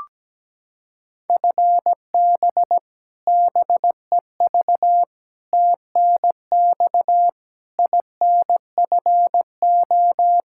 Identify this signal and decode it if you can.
Morse code, heard as a keyed tone, about 17 wpm, 700 Hz. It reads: FB BEV TNX INFO
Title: Practice QSO - Exchange 3 - More Info - 17wpm